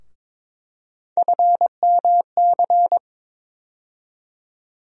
Morse code, heard as a keyed tone, about 21 wpm, 700 Hz. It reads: FMC
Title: USA Call Sign Postfixes - 3 Letters 22wpm